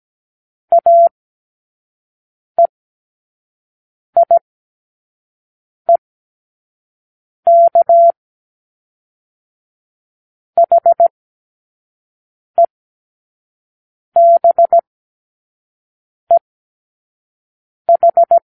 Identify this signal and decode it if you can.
Morse code, heard as a keyed tone, about 17 wpm, spaced as for 5 wpm, 680 Hz. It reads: AEIEK HEBEH
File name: k7qo / Lesson18(A-L)